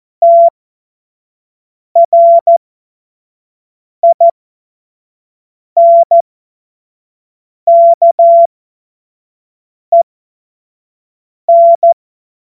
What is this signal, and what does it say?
Morse code, heard as a keyed tone, about 14 wpm, spaced as for 5 wpm, 680 Hz. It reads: TRINKEN
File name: Morsekurs / Witze / Witz 25. Januar 2022, 25 F70